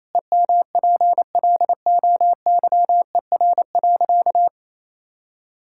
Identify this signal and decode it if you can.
Morse code, heard as a keyed tone, about 27 wpm, 700 Hz. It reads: EMPLOYER.
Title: Sentences from Top 1600 Words 28wpm